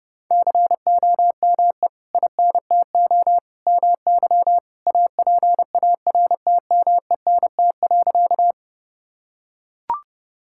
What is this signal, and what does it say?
Morse code, heard as a keyed tone, about 30 wpm, 700 Hz. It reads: COME INTO MY APARTMENT.
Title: Sentences from Top 1000 Words - Review Part 6 - 30wpm